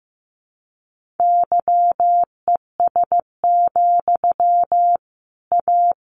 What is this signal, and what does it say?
Morse code, heard as a keyed tone, about 14 wpm, 700 Hz. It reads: YES, A
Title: Sentences from Top 1500 Words 15wpm